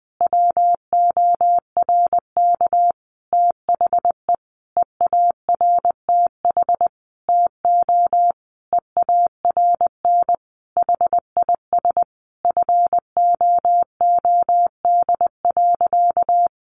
Morse code, heard as a keyed tone, about 20 wpm, 700 Hz. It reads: WORK THE EARTH TO EARN HIS FOOD.
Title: Sentences from Top 1500 Words 20wpm